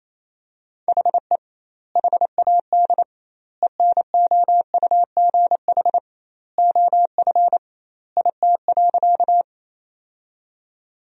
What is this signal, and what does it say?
Morse code, heard as a keyed tone, about 28 wpm, 700 Hz. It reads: HE HAD ENOUGH OF IT.
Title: 2x Word-Spacing - Sentences from Top 400 Words - 28wpm